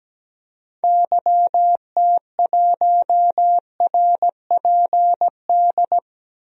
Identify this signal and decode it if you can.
Morse code, heard as a keyed tone, about 17 wpm, 700 Hz. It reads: YT1RPD
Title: DXCC-100 Call Signs - 6 Characters - 17wpm